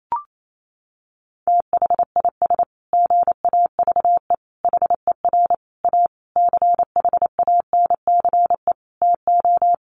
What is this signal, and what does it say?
Morse code, heard as a keyed tone, about 28 wpm, 700 Hz. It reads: THIS GAVE HER A CHANCE TO
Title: Sentences from Top 1400 Words 28wpm